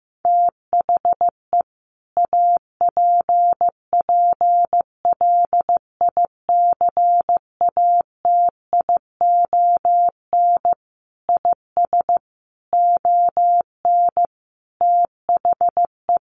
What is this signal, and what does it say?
Morse code, heard as a keyed tone, about 15 wpm, 700 Hz. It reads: THE APPLICATION IS ON THE
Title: Sentences from Top 1200 Words 15wpm